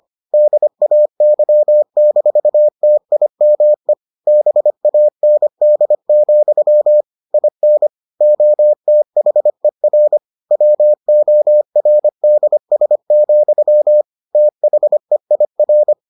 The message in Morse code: DAY-TIME BAND, IN OTHER WORDS, THEIR

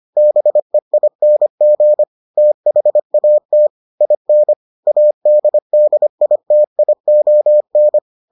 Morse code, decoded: BEING THAT IN ADDITION

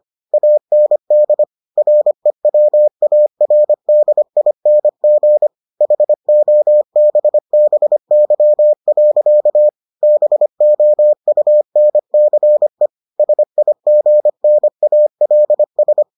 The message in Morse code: AND REWARDING HOBBY. BOUNCE SIGNALS

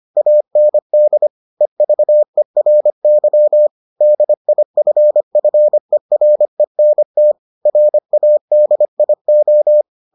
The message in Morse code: AND EVERY DIFFERENT RADIO

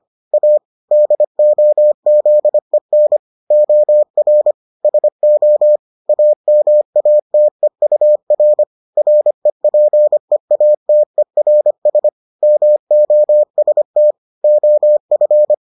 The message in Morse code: A DOZEN OR SO AMATEUR REPEATERS MOST OF

